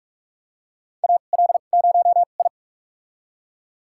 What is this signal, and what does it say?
Morse code, heard as a keyed tone, about 43 wpm, spaced as for 31 wpm, 700 Hz. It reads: AL0I
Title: USA Call Signs - 2x Character Spacing - 45wpm